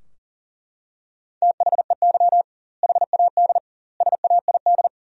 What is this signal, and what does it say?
Morse code, heard as a keyed tone, about 40 wpm, 700 Hz. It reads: THEY HAD SAID
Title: 2x Word-Spacing - Sentences from Top 400 Words - 40wpm